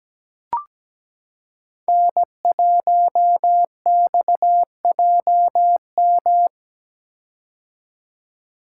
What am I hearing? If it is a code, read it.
Morse code, heard as a keyed tone, about 17 wpm, 700 Hz. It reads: N1XJM
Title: K1USN SST QSO Practice - 17wpm